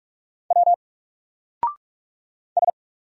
Morse code, heard as a keyed tone, about 44 wpm, 700 Hz. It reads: W S
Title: Single Character (Letter:Number:Punctuation:Prosign) 45wpm